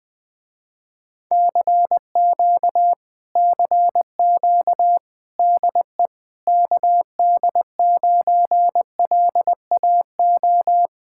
Morse code, heard as a keyed tone, about 20 wpm, 700 Hz. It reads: CQ CQ DE KD9LAO
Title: Speed-Racing Common QSO Phrases Encore - 20wpm